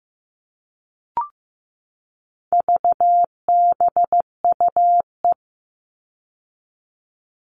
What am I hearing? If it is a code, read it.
Morse code, heard as a keyed tone, about 15 wpm, 700 Hz. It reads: VBUE
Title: All Cut Numbers - 4-Digit 15wpm